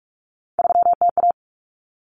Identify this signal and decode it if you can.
Morse code, heard as a keyed tone, about 44 wpm, 700 Hz. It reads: 3TA
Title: Common Cut Numbers - 3-Digit 45wpm